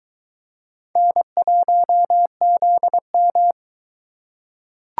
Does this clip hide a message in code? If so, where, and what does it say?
Yes: Morse code from 1.0 to 3.5 s, heard as a keyed tone, about 22 wpm, 700 Hz. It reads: N1ZM